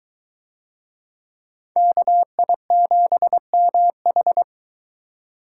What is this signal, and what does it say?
Morse code, heard as a keyed tone, about 22 wpm, 700 Hz. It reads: KI7MH